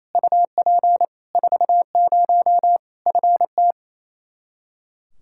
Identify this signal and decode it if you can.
Morse code, heard as a keyed tone, about 28 wpm, 700 Hz. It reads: UP 40 FT
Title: K1USN SST QSO Practice 28wpm